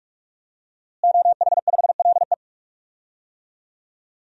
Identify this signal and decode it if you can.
Morse code, heard as a keyed tone, about 46 wpm, 700 Hz. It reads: OH5LE
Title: DXCC-100 Call Signs - 5 Characters - Repeated 2x - 45wpm